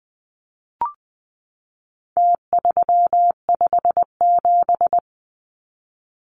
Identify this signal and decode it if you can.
Morse code, heard as a keyed tone, about 20 wpm, 700 Hz. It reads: T357
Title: Numbers - Contest Style - 4-Digit 20wpm